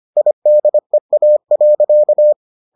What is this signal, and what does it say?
Morse code, heard as a keyed tone, about 25 wpm, 600 Hz. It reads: IDEA.